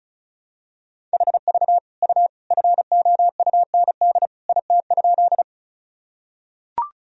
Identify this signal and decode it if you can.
Morse code, heard as a keyed tone, about 36 wpm, 700 Hz. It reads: HV U FOUND IT?
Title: Sentences from Top 300 Words and Abbreviations 35wpm